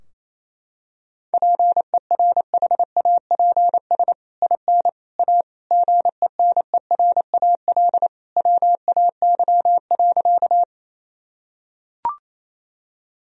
Morse code, heard as a keyed tone, about 28 wpm, 700 Hz. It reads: PERHAPS IN A GENERAL WAY.
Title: Sentences from Top 1000 Words - Review Part 3 - 28wpm